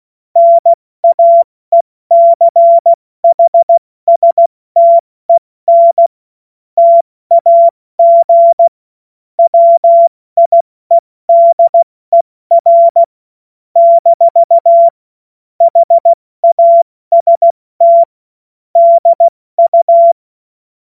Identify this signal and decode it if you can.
Morse code, heard as a keyed tone, about 16 wpm, spaced as for 14 wpm, 680 Hz. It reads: NAECHSTEN TAG WIEDER - HAST DU
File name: Morsekurs / Witze / Witz 25. Mai 2021, 70 F80